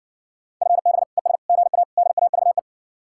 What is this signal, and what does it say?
Morse code, heard as a keyed tone, about 60 wpm, 700 Hz. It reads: FB ES GA DAVE